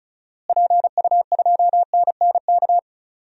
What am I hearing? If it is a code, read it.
Morse code, heard as a keyed tone, about 36 wpm, 700 Hz. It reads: PU2NNK
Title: DXCC-100 Call Signs - 6 Characters - Repeated 2x - 35wpm